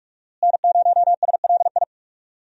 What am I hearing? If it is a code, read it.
Morse code, heard as a keyed tone, about 45 wpm, 700 Hz. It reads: N0SLI
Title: USA Call Signs - 5 Characters - Repeated 2x - 45wpm